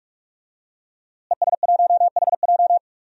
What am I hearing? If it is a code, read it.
Morse code, heard as a keyed tone, about 46 wpm, 700 Hz. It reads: ES1HJ
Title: DXCC-100 Call Signs - 5 Characters - Repeated 3x - 45wpm